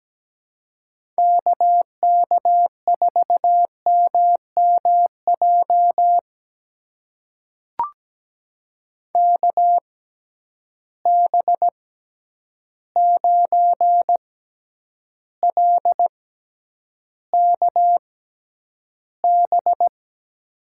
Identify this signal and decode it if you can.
Morse code, heard as a keyed tone, about 17 wpm, spaced as for 9 wpm, 700 Hz. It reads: KK4MMJ KB9LKB
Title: USA Call Signs - 6x Character Spacing - 17wpm